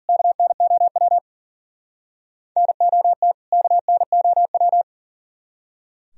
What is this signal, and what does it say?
Morse code, heard as a keyed tone, about 52 wpm, 700 Hz. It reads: KNOW NOT KNOW